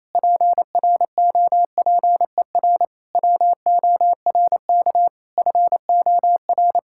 Morse code, heard as a keyed tone, about 28 wpm, 700 Hz. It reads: PROPER WORK FOR
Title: Sentences from Top 1000 Words - Review Part 5 - 28wpm